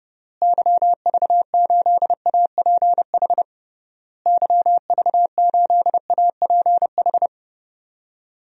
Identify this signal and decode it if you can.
Morse code, heard as a keyed tone, about 30 wpm, 700 Hz. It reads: YV8APH YV8APH
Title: DXCC-100 Call Signs - 6 Characters - Repeated 3x - 30wpm